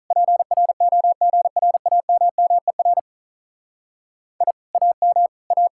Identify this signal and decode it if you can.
Morse code, heard as a keyed tone, about 45 wpm, 700 Hz. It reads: PROGRAMMER I AM A